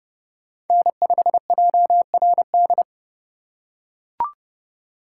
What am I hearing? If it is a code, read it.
Morse code, heard as a keyed tone, about 29 wpm, 700 Hz. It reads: N5JRD